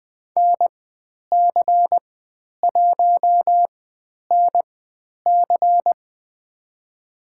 Morse code, heard as a keyed tone, about 20 wpm, spaced as for 10 wpm, 700 Hz. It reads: NC1NC
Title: 095 Lesson 16g - Letter c - Call Signs 20wpm